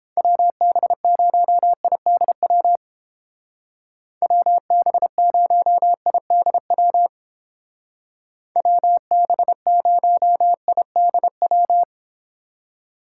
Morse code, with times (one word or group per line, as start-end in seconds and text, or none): 0.2-2.8 WB0IDW
4.2-7.1 WB0IDW
8.6-11.8 WB0IDW